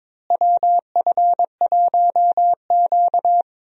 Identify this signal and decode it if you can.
Morse code, heard as a keyed tone, about 22 wpm, 700 Hz. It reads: WF1Q
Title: USA Call Signs - 4 Characters - Repeated 3x - 22wpm